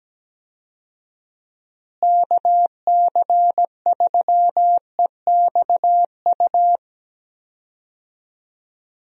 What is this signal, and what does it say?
Morse code, heard as a keyed tone, about 17 wpm, 700 Hz. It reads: KC3EXU
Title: USA Call Signs - 6 Characters - Repeated 2x - 17wpm